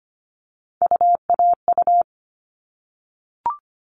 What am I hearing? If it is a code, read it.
Morse code, heard as a keyed tone, about 25 wpm, 700 Hz. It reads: UAU
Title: All Cut Numbers - 3-Digit 25wpm